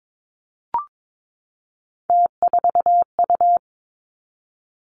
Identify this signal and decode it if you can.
Morse code, heard as a keyed tone, about 22 wpm, 700 Hz. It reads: T4U
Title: Three Challenging Letter-Number Combinations 22wpm